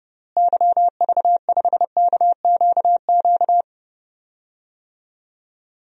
Morse code, heard as a keyed tone, about 30 wpm, 700 Hz. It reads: YV5KQQ
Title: DXCC-100 Call Signs - 6 Characters - Repeated 3x - 30wpm